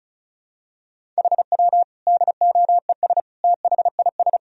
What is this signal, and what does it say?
Morse code, heard as a keyed tone, about 36 wpm, 700 Hz. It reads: HW DOES THIS
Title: Sentences from Top 400 Words and Abbreviations 35wpm